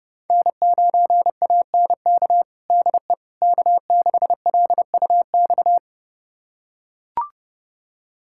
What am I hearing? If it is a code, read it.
Morse code, heard as a keyed tone, about 32 wpm, 700 Hz. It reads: N9ANK DE K6LUX
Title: USA Callsign DE Callsign 30wpm